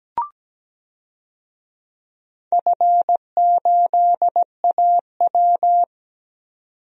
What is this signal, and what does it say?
Morse code, heard as a keyed tone, about 17 wpm, 700 Hz. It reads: F8AW
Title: DXCC-100 Call Signs - 4 Characters - 17wpm